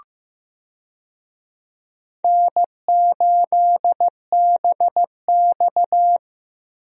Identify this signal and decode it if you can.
Morse code, heard as a keyed tone, about 15 wpm, 700 Hz. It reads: N8BX
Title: USA Call Signs - 4 Characters - Repeated 3x - 15wpm